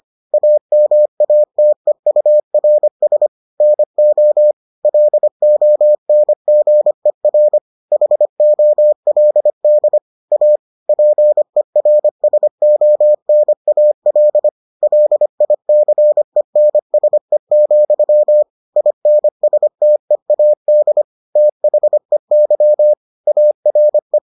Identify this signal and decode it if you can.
Morse code, heard as a keyed tone, about 25 wpm, 600 Hz. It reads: AMATEURS NO LONGER HOLD A PERSONAL LICENSE, INSTEAD THEY ARE